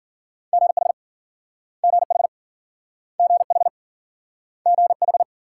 Morse code, heard as a keyed tone, about 53 wpm, 700 Hz. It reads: GH GH GH GH